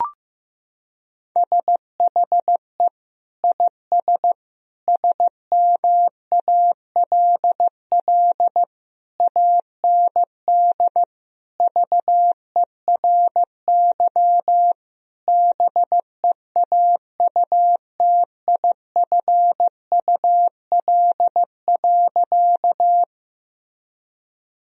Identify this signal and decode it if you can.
Morse code, heard as a keyed tone, about 16 wpm, 700 Hz. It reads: SHE IS SMALL AND VERY BEAUTIFUL.
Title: Sentences from Top 1000 Words - Review Part 1 - 15wpm